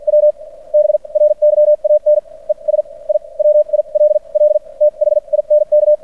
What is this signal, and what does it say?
Morse code, heard as a keyed tone, about 53 wpm, spaced as for 40 wpm, 600 Hz. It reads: W BRYAN ES I WILL THINK
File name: archived / CRSnet